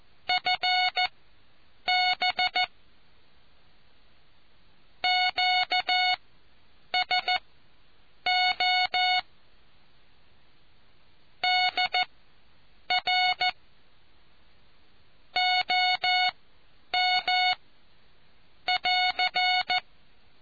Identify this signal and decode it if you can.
Morse code, heard as a keyed tone, about 14 wpm, spaced as for 8 wpm, 730 Hz. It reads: FB QSO DR OM <AR>